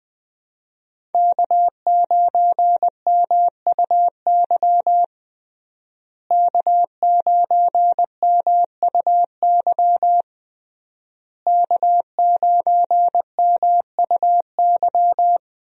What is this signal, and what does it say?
Morse code, heard as a keyed tone, about 20 wpm, 700 Hz. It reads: K9MUY K9MUY K9MUY
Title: USA Call Signs - 5 Characters - Repeated 3x - 20wpm